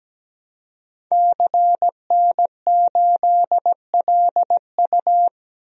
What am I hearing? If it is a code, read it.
Morse code, heard as a keyed tone, about 17 wpm, 700 Hz. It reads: CN8LU
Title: DXCC-100 Call Signs - 5 Characters - Repeated 2x - 17wpm